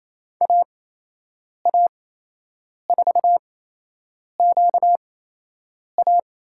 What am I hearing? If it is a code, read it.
Morse code, heard as a keyed tone, about 27 wpm, spaced as for 8 wpm, 700 Hz. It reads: AA4QA
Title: USA Call Signs - 5 Characters - 8x Character Spacing - 28wpm